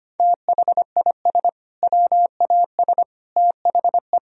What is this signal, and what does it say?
Morse code, heard as a keyed tone, about 25 wpm, 700 Hz. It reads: THIS WAS THE
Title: Sentences from Top 1000 Words - Review Part 1 - 25wpm